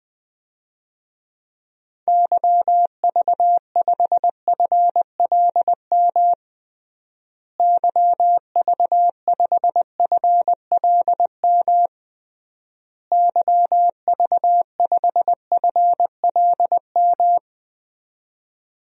Morse code, heard as a keyed tone, about 21 wpm, 700 Hz. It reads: YV5FLM YV5FLM YV5FLM
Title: DXCC-100 Call Signs - 6 Characters - Repeated 3x - 20wpm